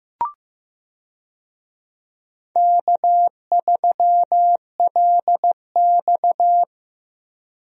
Morse code, heard as a keyed tone, about 15 wpm, 700 Hz. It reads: K3LX
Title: USA Call Signs - 4 Characters - 15wpm